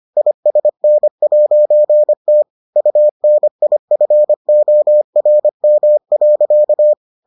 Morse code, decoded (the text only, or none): ISN'T UNIFORM.